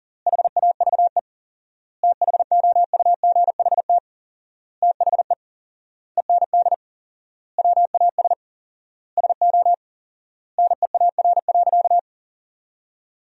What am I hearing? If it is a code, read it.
Morse code, heard as a keyed tone, about 37 wpm, 700 Hz. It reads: HAVE THOUGHT THE END WAS SO NEAR.
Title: 4x Word-Spacing - Sentences from Top 300 Words - 40wpm